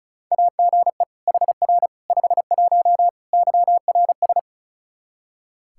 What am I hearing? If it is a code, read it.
Morse code, heard as a keyed tone, about 34 wpm, 700 Hz. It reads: AGE HR 51 YRS
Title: Practice QSO - Exchange 3 - More Info - 35wpm